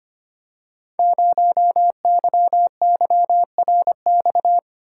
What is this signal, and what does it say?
Morse code, heard as a keyed tone, about 25 wpm, 700 Hz. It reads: 0YYRX
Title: Code Groups - 5 Letter:Numbers - 25wpm